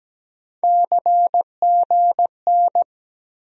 Morse code, heard as a keyed tone, about 17 wpm, 700 Hz. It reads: CGN
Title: USA Call Sign Postfixes - 3 Letters 17wpm